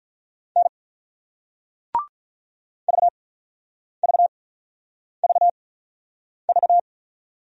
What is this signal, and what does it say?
Morse code, heard as a keyed tone, about 47 wpm, spaced as for 9 wpm, 700 Hz. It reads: N VVVV